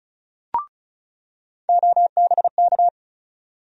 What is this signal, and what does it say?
Morse code, heard as a keyed tone, about 36 wpm, 700 Hz. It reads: OBK
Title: USA Call Sign Postfixes - 3 Letters 35wpm